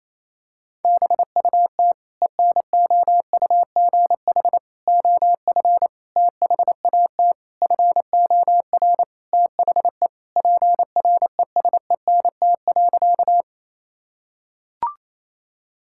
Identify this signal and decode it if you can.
Morse code, heard as a keyed tone, about 28 wpm, 700 Hz. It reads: BUT ENOUGH OF THAT FOR THE PRESENT.
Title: Sentences from Top 1000 Words - Review Part 5 - 28wpm